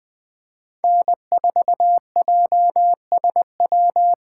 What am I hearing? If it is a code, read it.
Morse code, heard as a keyed tone, about 20 wpm, 700 Hz. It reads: N4JSW
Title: USA Call Signs - 5 Characters - 8x Character Spacing - 20wpm